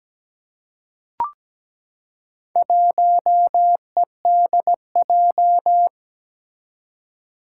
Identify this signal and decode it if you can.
Morse code, heard as a keyed tone, about 17 wpm, 700 Hz. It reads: 1EDJ
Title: Four Challenging Letter-Number Combinations 17wpm